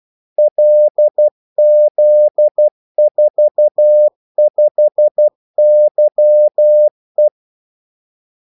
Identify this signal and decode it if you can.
Morse code, heard as a keyed tone, about 12 wpm, 600 Hz. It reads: LZ45YE
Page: 2 ekstra ordmellemrum mellem kaldesignalerne